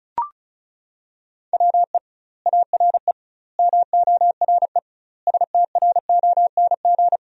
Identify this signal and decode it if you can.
Morse code, heard as a keyed tone, about 36 wpm, 700 Hz. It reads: WE ARE MORE STRONG
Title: 2x Word-Spacing - Sentences from Top 500 Words - 35wpm